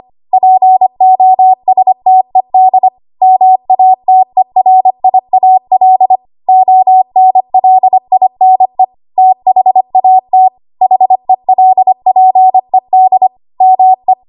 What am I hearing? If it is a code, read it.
Morse code, heard as a keyed tone, about 25 wpm, 750 Hz. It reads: POSTED MATERIAL ONLINE THAT HELPED ME
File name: Archive / 25 WPM / 25WPM